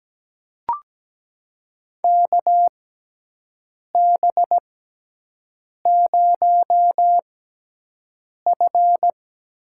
Morse code, heard as a keyed tone, about 17 wpm, spaced as for 6 wpm, 700 Hz. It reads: KB0F
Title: USA Call Signs - 6 Characters - 6x Character Spacing - 17wpm